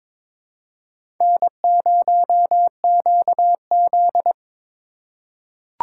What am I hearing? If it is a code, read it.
Morse code, heard as a keyed tone, about 21 wpm, 700 Hz. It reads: N0QZ
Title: USA Call Signs - 4 Characters - 4x Character Spacing - 22wpm